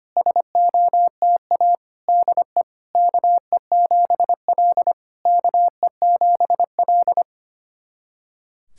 Morse code, heard as a keyed tone, about 25 wpm, 690 Hz. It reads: SOTA DE KE7L KE7L
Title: SOTA QSO Practice 25wpm